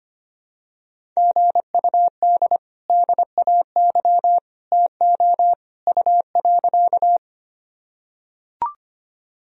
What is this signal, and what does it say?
Morse code, heard as a keyed tone, about 25 wpm, 700 Hz. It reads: GUD DAY TO U.
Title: Sentences from Top 200 Words and Abbreviations 25wpm